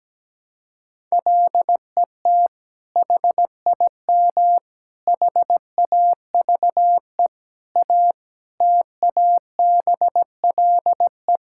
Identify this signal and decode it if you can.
Morse code, heard as a keyed tone, about 17 wpm, 700 Hz. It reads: LET HIM HAVE A TABLE
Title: Sentences from Top 1600 Words 17wpm